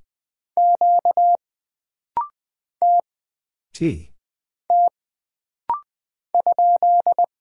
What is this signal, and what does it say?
Morse code, heard as a keyed tone, about 20 wpm, 700 Hz. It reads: Q T T ?
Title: Single Character (Letter:Number:Punctuation:Prosign) - Rapid-Fire - 20wpm